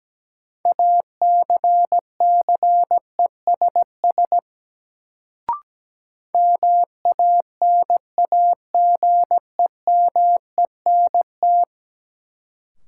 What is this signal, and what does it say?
Morse code, heard as a keyed tone, about 17 wpm, 700 Hz. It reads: ACCESS MANAGEMENT
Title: Sentences from Top 1000 Words - Review Part 4 - New Word Drill - 17wpm